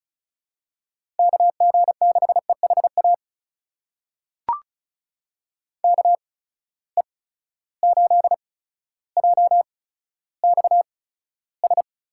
Morse code, heard as a keyed tone, about 34 wpm, spaced as for 15 wpm, 700 Hz. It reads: KG6EHA KE8JXS
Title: USA Call Signs - 6 Characters - 8x Character Spacing - 35wpm